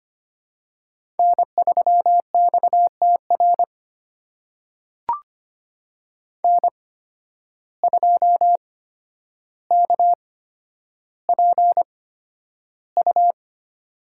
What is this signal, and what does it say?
Morse code, heard as a keyed tone, about 25 wpm, spaced as for 11 wpm, 700 Hz. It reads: N3XTR N2KPU
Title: USA Call Signs - 5 Characters - 8x Character Spacing - 25wpm